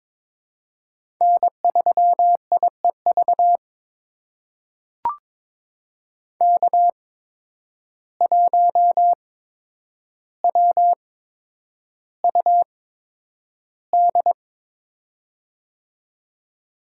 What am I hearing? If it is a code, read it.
Morse code, heard as a keyed tone, about 22 wpm, spaced as for 10 wpm, 700 Hz. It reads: N3IEV K1WUD
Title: USA Call Signs - 8x Character Spacing - 22wpm